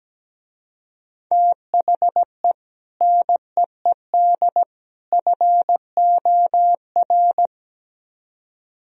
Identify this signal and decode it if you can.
Morse code, heard as a keyed tone, about 17 wpm, 700 Hz. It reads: THE NEED FOR
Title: Speed-Racing - Sets of 3 Words 17wpm